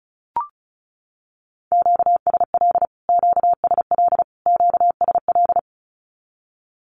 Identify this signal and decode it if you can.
Morse code, heard as a keyed tone, about 34 wpm, 700 Hz. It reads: QSL QSL QSL
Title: Q-Codes Repeated 3x 35wpm